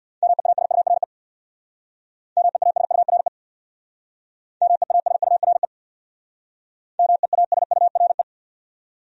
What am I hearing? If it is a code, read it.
Morse code, heard as a keyed tone, about 64 wpm, 700 Hz. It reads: MEASURE MEASURE MEASURE MEASURE